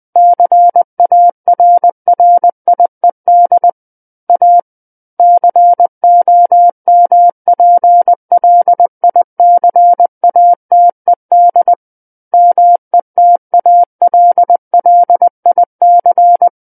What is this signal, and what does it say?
Morse code, heard as a keyed tone, about 20 wpm, 700 Hz. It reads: CARRIED A COMPLICATED METALLIC
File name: 20 WPM